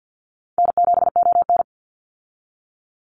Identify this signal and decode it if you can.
Morse code, heard as a keyed tone, about 49 wpm, 700 Hz. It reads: N7ON
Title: Call Signs Repeated 3x - 50wpm